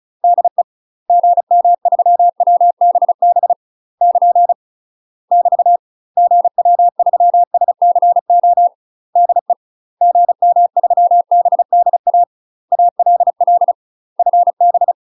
Morse code, heard as a keyed tone, about 36 wpm, 700 Hz. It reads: DE GM3WBB <KN> <BT> GW3SCO DE GM3BDA ALL FB
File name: UK MASTER QSOs TRAINING/35WPM